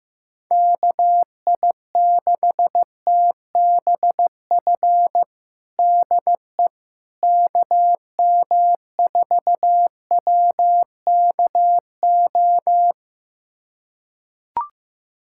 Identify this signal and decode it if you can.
Morse code, heard as a keyed tone, about 15 wpm, 700 Hz. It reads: KI6TBF DE KM4WKO
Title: USA Callsign DE Callsign 15wpm